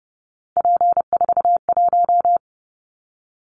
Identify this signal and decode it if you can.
Morse code, heard as a keyed tone, about 30 wpm, 700 Hz. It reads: P41
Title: Callsign Prefixes for Top 100 DXCC - 30wpm